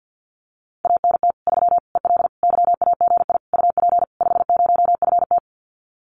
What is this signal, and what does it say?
Morse code, heard as a keyed tone, about 50 wpm, 700 Hz. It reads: ANT 3 EL YAGI UP 50FT
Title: Common QSO Phrases Encore 50wpm